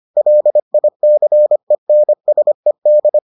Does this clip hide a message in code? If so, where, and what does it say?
Yes: Morse code from 0.2 to 3.2 s, heard as a keyed tone, about 25 wpm, 600 Hz. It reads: LICENSED